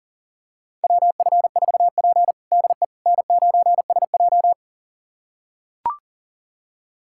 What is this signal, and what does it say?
Morse code, heard as a keyed tone, about 40 wpm, 700 Hz. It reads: WF4P DE N9SJ
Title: USA Callsign DE Callsign - 4 Characters - 40wpm